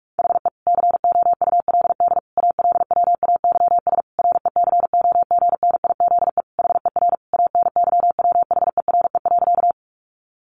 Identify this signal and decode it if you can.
Morse code, heard as a keyed tone, about 44 wpm, 700 Hz. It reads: HE COULD ALWAYS RECOGNIZE HER ANYWHERE.
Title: Sentences from Top 1400 Words 45wpm